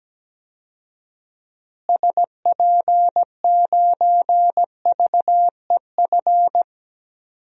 Morse code, heard as a keyed tone, about 17 wpm, 700 Hz. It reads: SP9VEF
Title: DXCC-100 Call Signs - 6 Characters - Repeated 2x - 17wpm